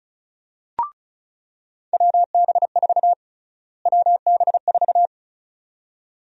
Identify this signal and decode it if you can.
Morse code, heard as a keyed tone, about 34 wpm, 700 Hz. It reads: WB4 WB4
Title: USA Call Sign Prefixes - Repeated 2x - 35wpm